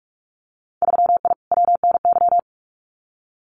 Morse code, heard as a keyed tone, about 46 wpm, 700 Hz. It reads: 3I WNY
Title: Winter Field Day Exchanges 45wpm